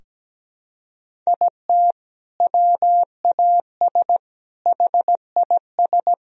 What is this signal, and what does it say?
Morse code, heard as a keyed tone, about 17 wpm, 700 Hz. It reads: IT WAS HIS
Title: Sentences from Top 1000 Words - Review Part 2 - 17wpm